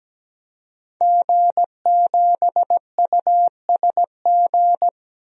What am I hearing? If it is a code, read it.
Morse code, heard as a keyed tone, about 17 wpm, 700 Hz. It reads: G7USG
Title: DXCC-100 Call Signs - 5 Characters - 17wpm